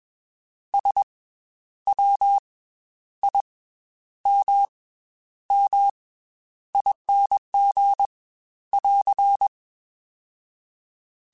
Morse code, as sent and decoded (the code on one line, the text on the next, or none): ... / .-- / .. / -- / -- / .. -. --. / .-.-.
S W I M M ING <AR>